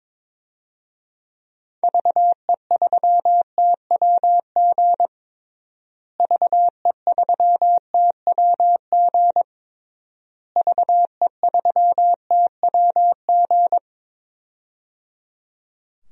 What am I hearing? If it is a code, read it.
Morse code, heard as a keyed tone, about 22 wpm, 700 Hz. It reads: VE3TWG VE3TWG VE3TWG
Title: DXCC-100 Call Signs - 6 Characters - Repeated 3x - 22wpm